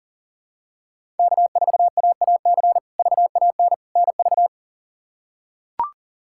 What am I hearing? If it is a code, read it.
Morse code, heard as a keyed tone, about 40 wpm, 700 Hz. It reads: K4AAC VAN NV
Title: K1USN SST QSO Practice - 40wpm